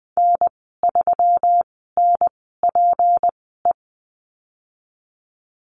Morse code, heard as a keyed tone, about 20 wpm, spaced as for 15 wpm, 700 Hz. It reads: N3NPE
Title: 179 Lesson 29g - Slash - Call Signs 20wpm